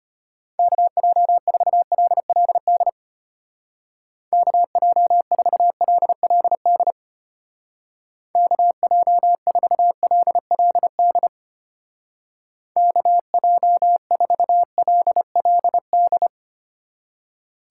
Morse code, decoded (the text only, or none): KJ4LLD KJ4LLD KJ4LLD KJ4LLD